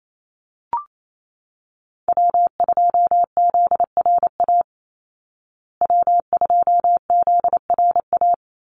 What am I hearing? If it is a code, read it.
Morse code, heard as a keyed tone, about 28 wpm, 700 Hz. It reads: W2ZRA W2ZRA
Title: Call Signs Repeated 2x - 28wpm